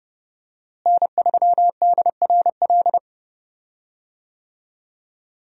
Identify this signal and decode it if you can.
Morse code, heard as a keyed tone, about 30 wpm, 700 Hz. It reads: N3DRL
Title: USA Call Signs - 5 Characters - Repeated 2x - 30wpm